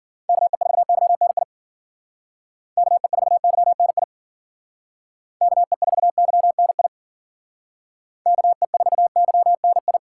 Morse code, heard as a keyed tone, about 50 wpm, 700 Hz. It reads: KE4YNI KE4YNI KE4YNI KE4YNI